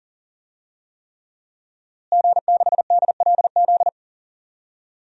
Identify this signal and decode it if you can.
Morse code, heard as a keyed tone, about 40 wpm, 700 Hz. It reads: G6DLZ
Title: DXCC-100 Call Signs - 5 Characters - Repeated 3x - 40wpm